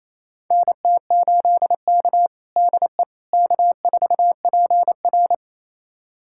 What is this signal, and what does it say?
Morse code, heard as a keyed tone, about 27 wpm, 690 Hz. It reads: NT8K DE K4PR
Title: USA Callsign DE Callsign - 4 Characters - 28wpm